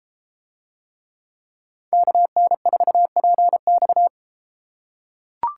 Morse code, heard as a keyed tone, about 31 wpm, 700 Hz. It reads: KN4PX